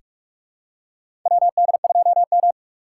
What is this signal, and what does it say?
Morse code, heard as a keyed tone, about 44 wpm, 700 Hz. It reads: WD2M
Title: USA Call Signs - 4 Characters - 4x Character Spacing - 45wpm